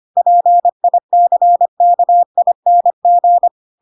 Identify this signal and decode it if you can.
Morse code, heard as a keyed tone, about 25 wpm, 700 Hz. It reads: PICKING
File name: Pinocchio / Pinocchio ch7 25 25 ik1hge